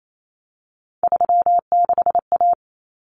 Morse code, heard as a keyed tone, about 28 wpm, 700 Hz. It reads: 36A
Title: Common Cut Numbers - 3-Digit 28wpm